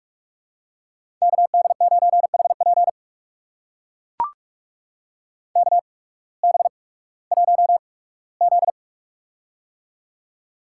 Morse code, heard as a keyed tone, about 44 wpm, spaced as for 21 wpm, 700 Hz. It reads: KD9HP KB1Z
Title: USA Call Signs - 8x Character Spacing - 45wpm